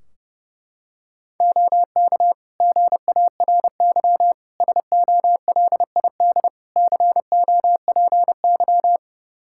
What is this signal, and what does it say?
Morse code, heard as a keyed tone, about 32 wpm, 700 Hz. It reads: OK GARY SOLID COPY
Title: Practice QSO - Exchange 3 - More Info - 30wpm